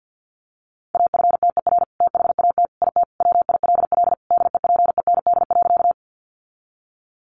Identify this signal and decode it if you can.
Morse code, heard as a keyed tone, about 50 wpm, 700 Hz. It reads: AFTER THAT IT WILL DEPEND.
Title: Sentences from Top 1300 Words 50wpm